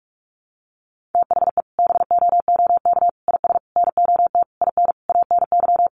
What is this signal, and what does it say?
Morse code, heard as a keyed tone, about 44 wpm, 700 Hz. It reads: THE BOOK IS NOT IN ANY
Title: Sentences from Top 1300 Words 45wpm